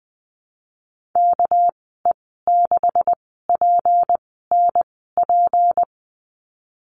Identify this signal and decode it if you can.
Morse code, heard as a keyed tone, about 20 wpm, spaced as for 15 wpm, 700 Hz. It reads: KE6PNP
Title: 200 Lesson 32g - Letter k - Call Signs 20wpm